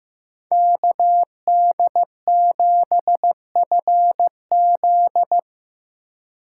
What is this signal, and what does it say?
Morse code, heard as a keyed tone, about 15 wpm, 700 Hz. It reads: KD7FZ
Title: USA Call Signs - 5 Characters - 6x Character Spacing - 15wpm